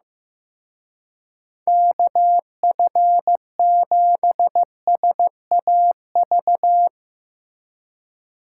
Morse code, heard as a keyed tone, about 17 wpm, 700 Hz. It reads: KF7SAV
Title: Speed Racing - USA 2x3 Call Signs 15wpm